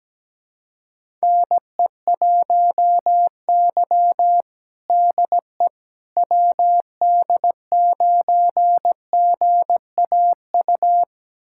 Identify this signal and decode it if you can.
Morse code, heard as a keyed tone, about 17 wpm, 700 Hz. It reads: NE1Y DE WD9GAU
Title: USA Callsign DE Callsign 17wpm